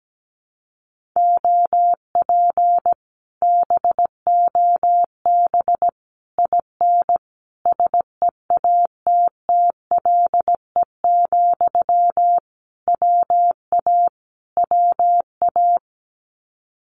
Morse code, heard as a keyed tone, about 17 wpm, 700 Hz. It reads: OP BOB IN SEATTLE, WA WA
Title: Common QSO Phrases Encore 17wpm